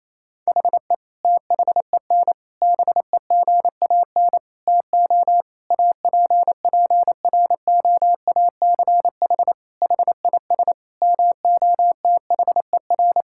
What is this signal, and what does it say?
Morse code, heard as a keyed tone, about 28 wpm, 700 Hz. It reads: HE THEN BEGAN TO APPROACH HIS MOTHER
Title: Sentences from Top 1000 Words - Review Part 6 - 28wpm